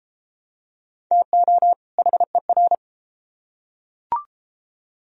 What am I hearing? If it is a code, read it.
Morse code, heard as a keyed tone, about 31 wpm, 700 Hz. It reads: TO HER